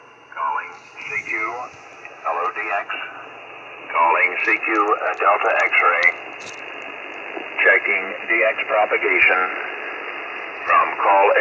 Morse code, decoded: none